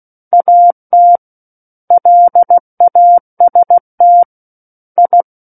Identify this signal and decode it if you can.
Morse code, heard as a keyed tone, about 16 wpm, 700 Hz. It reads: AT LAST I
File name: Book One / 07-Chapter 7 - 16 WPM